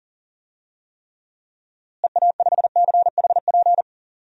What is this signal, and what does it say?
Morse code, heard as a keyed tone, about 37 wpm, 700 Hz. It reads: EA5CHP
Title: DXCC-100 Call Signs - 6 Characters - Repeated 2x - 40wpm